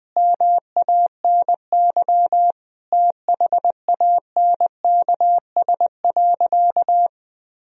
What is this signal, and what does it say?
Morse code, heard as a keyed tone, about 20 wpm, 700 Hz. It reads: MANY THANKS.
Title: Sentences from Top 1000 Words - Review Part 3 - 20wpm